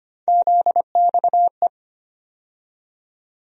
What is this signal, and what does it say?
Morse code, heard as a keyed tone, about 25 wpm, 700 Hz. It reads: ZXE
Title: USA Call Sign Postfixes - 3 Letters 25wpm